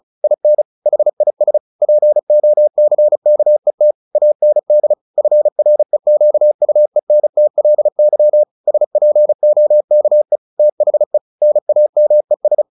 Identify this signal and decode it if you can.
Morse code, heard as a keyed tone, about 34 wpm, 600 Hz. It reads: IN HIS POCKET AND FREQUENTLY SPOKE THE NAMES
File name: the old man and the sea / the old man and the sea 1 35wpm